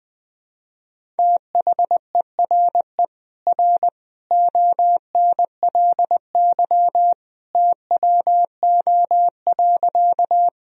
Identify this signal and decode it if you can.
Morse code, heard as a keyed tone, about 20 wpm, 700 Hz. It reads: THERE R ONLY TWO.
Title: Sentences from Top 200 Words and Abbreviations 20wpm